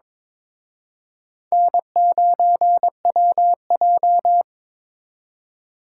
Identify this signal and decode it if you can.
Morse code, heard as a keyed tone, about 26 wpm, 700 Hz. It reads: N9WJ
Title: Speed Racing - USA 1x2 Call Signs 22wpm